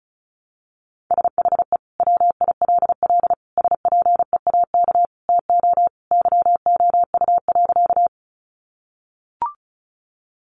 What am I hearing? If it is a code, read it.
Morse code, heard as a keyed tone, about 34 wpm, 700 Hz. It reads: SHE WILL SPEAK TO YOU.
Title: Sentences from Top 1100 Words 35wpm